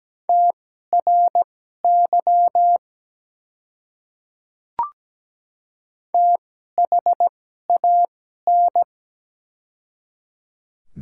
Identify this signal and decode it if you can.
Morse code, heard as a keyed tone, about 17 wpm, spaced as for 12 wpm, 700 Hz. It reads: TRY THAN
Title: 2x Character Spacing - Top 200 Words - 17wpm